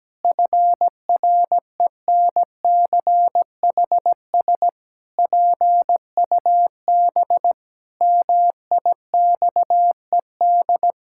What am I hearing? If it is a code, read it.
Morse code, heard as a keyed tone, about 17 wpm, 700 Hz. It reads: FRENCHS PUB MIXED
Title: Pangram Challenge v2 - 17wpm